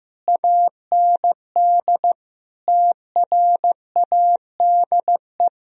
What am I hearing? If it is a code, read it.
Morse code, heard as a keyed tone, about 15 wpm, 700 Hz. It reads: AND TRADE
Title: Sentences from Top 1000 Words - Review Part 4 - 15wpm